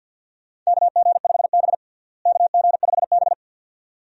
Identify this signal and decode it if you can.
Morse code, heard as a keyed tone, about 52 wpm, 700 Hz. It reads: KG5B KG5B
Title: USA Call Signs - 4 Characters - Repeated 2x - 50wpm